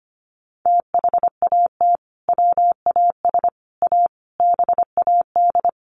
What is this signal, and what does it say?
Morse code, heard as a keyed tone, about 25 wpm, 700 Hz. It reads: THAT WAS A BAD
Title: Sentences from Top 800 Words 25wpm